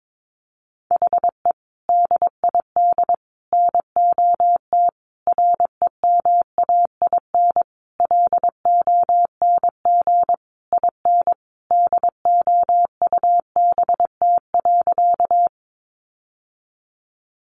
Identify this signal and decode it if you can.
Morse code, heard as a keyed tone, about 22 wpm, 700 Hz. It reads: HE DID NOT REMAIN LONG IN DOUBT.
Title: Sentences from Top 1500 Words 22wpm